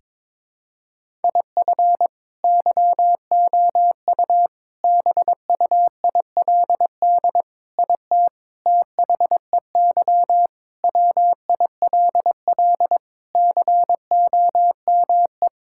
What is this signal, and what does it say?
Morse code, heard as a keyed tone, about 22 wpm, 700 Hz. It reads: IF YOU BUILD IT THEY WILL COME
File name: practice / 22wpm / Sets of 7 Words - 22wpm